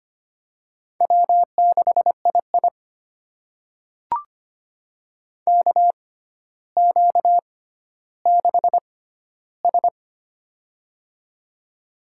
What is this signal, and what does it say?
Morse code, heard as a keyed tone, about 25 wpm, spaced as for 13 wpm, 700 Hz. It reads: W6II KQ6S
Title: USA Call Signs - 4 Characters - 6x Character Spacing - 25wpm